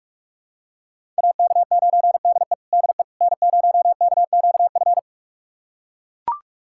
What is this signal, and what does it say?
Morse code, heard as a keyed tone, about 42 wpm, 700 Hz. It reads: AK9DE DE N0KQF
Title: USA Callsign DE Callsign - 5 Characters - 45wpm